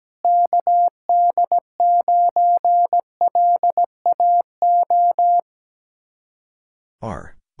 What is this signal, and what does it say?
Morse code, heard as a keyed tone, about 17 wpm, 700 Hz. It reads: KD9LAO
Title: Speed-Racing Common QSO Phrases Encore - 17wpm